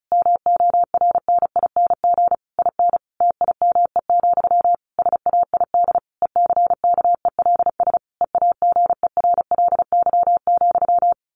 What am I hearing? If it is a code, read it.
Morse code, heard as a keyed tone, about 34 wpm, 700 Hz. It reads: MORNING IN TIME, SAID ECKELS EAGERLY,
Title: A Sound Of Thunder 35wpm